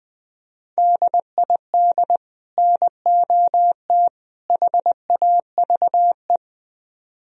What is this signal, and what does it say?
Morse code, heard as a keyed tone, about 23 wpm, 700 Hz. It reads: DID NOT HAVE
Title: Speed-Racing - Sets of 3 Words 20wpm